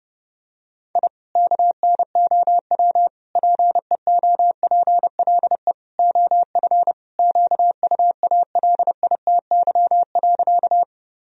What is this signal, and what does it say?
Morse code, heard as a keyed tone, about 32 wpm, 700 Hz. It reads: I KNOW PEOPLE OF QUALITY.
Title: Sentences from Top 1000 Words - Review Part 4 - 30wpm